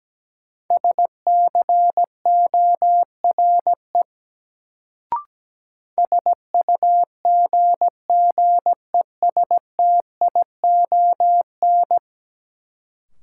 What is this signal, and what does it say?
Morse code, heard as a keyed tone, about 17 wpm, 700 Hz. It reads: SCORE SUGGESTION
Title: Sentences from Top 1600 Words - New Word Drill - 17wpm